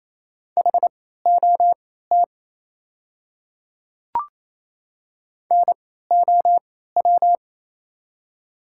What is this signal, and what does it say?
Morse code, heard as a keyed tone, about 28 wpm, spaced as for 16 wpm, 700 Hz. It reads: HOT NOW
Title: 3x Character Spacing - Top 100 Words - 28wpm